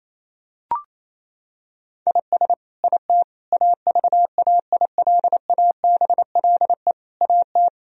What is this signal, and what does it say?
Morse code, heard as a keyed tone, about 28 wpm, 700 Hz. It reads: IS IT AVAILABLE AT
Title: Sentences from Top 1000 Words - Review Part 3 - 28wpm